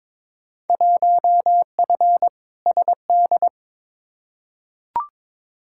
Morse code, heard as a keyed tone, about 22 wpm, 700 Hz. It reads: 1F SD
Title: Field Day Exchanges - 22wpm